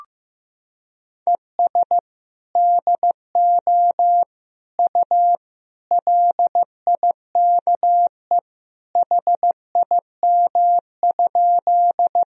Morse code, heard as a keyed tone, about 15 wpm, 700 Hz. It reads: ES DO U LIKE HIM?
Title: Sentences from Top 100 Words and Abbreviations 15wpm